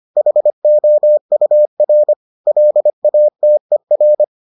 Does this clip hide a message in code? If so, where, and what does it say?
Yes: Morse code from 0.2 to 4.2 s, heard as a keyed tone, about 25 wpm, 600 Hz. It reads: HOUR LATER